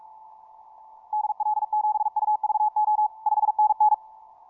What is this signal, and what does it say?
Morse code, heard as a keyed tone, about 46 wpm, 850 Hz. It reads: DL7UVO 5NN
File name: simple-file-list / tx5k 15mcw2